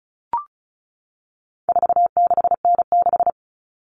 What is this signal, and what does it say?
Morse code, heard as a keyed tone, about 34 wpm, 700 Hz. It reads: 46N6
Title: Common Cut Numbers - 4-Digit 35wpm